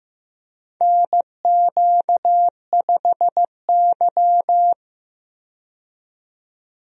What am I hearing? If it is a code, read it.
Morse code, heard as a keyed tone, about 15 wpm, 700 Hz. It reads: NQ5Y
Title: USA Call Signs - 4 Characters - 15wpm